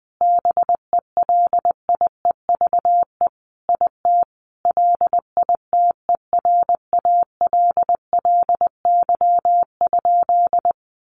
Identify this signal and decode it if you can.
Morse code, heard as a keyed tone, about 20 wpm, 700 Hz. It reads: BELIEVE IT LITERALLY?
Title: Sentences from Top 1400 Words 20wpm